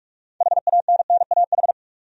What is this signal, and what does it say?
Morse code, heard as a keyed tone, about 45 wpm, 700 Hz. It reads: HANNAH
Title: USA Common Names - 45wpm